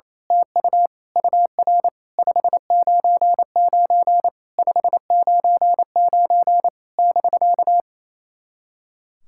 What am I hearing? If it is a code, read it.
Morse code, heard as a keyed tone, about 28 wpm, 700 Hz. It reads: TU UR 599 599 <BK>
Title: SOTA QSO Practice 28wpm